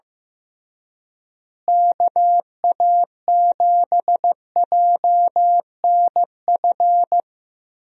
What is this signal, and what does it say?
Morse code, heard as a keyed tone, about 16 wpm, 700 Hz. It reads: KA7JNF
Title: USA Call Signs - 6 Characters - Repeated 2x - 15wpm